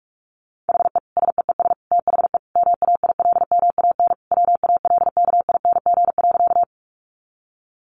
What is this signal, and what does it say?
Morse code, heard as a keyed tone, about 44 wpm, 700 Hz. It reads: HE SEES THE MAILMAN WALKING.
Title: Easy Phrases 45wpm